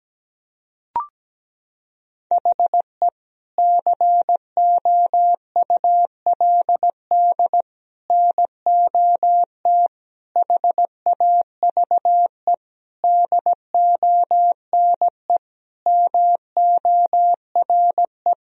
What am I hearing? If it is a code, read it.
Morse code, heard as a keyed tone, about 17 wpm, 700 Hz. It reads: HE COULD NOT HAVE DONE MORE
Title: Sentences from Top 1000 Words - Review Part 2 - 17wpm